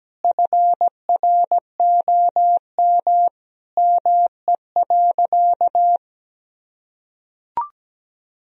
Morse code, heard as a keyed tone, about 17 wpm, 700 Hz. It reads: FROM ME.
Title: Sentences from Top 1000 Words - Review Part 2 - 17wpm